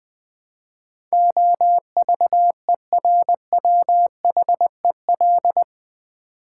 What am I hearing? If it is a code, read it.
Morse code, heard as a keyed tone, about 20 wpm, 700 Hz. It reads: OVERWHEL
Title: Build A Word - 4 Letters At A Time - 12-Letter Words - 20wpm